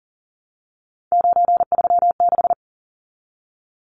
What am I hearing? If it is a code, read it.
Morse code, heard as a keyed tone, about 40 wpm, 700 Hz. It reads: 936
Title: Numbers - 3-Digit 40wpm